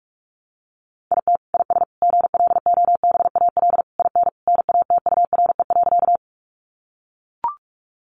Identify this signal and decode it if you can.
Morse code, heard as a keyed tone, about 44 wpm, 700 Hz. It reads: IT IS GLOBAL IN NATURE.
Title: Sentences from Top 1200 Words 45wpm